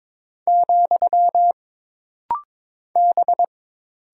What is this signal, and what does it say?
Morse code, heard as a keyed tone, about 22 wpm, 700 Hz. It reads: , B
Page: This rapid-fire timing is geared to push your ICR (Instant Character Recognition) skills.